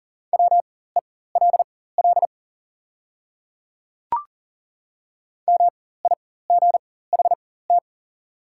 Morse code, heard as a keyed tone, about 40 wpm, spaced as for 19 wpm, 700 Hz. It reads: WELL MIGHT
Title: 4x Character Spacing - Top 300 Words - 40wpm